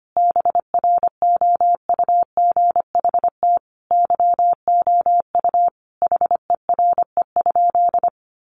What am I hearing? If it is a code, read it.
Morse code, heard as a keyed tone, about 25 wpm, 700 Hz. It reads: BROUGHT YOU HERE?
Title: Sentences from Top 500 Words 25wpm